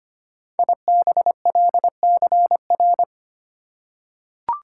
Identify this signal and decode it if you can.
Morse code, heard as a keyed tone, about 25 wpm, 690 Hz. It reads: IBLCR
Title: Code Groups - 5 Letter:Numbers - 25wpm